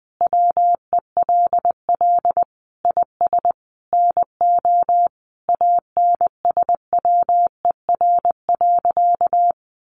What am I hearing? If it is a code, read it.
Morse code, heard as a keyed tone, about 20 wpm, 700 Hz. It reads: WELL IS NO ANSWER.
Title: Sentences from Top 1000 Words 20wpm